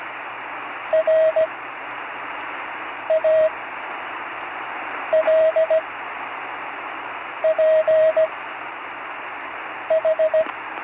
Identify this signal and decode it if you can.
Morse code, heard as a keyed tone, about 17 wpm, spaced as for 5 wpm, 630 Hz, with radio noise. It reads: RALPH
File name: CW Learning Programs / Learning Morse Code (from Tune in the World) / 35 answers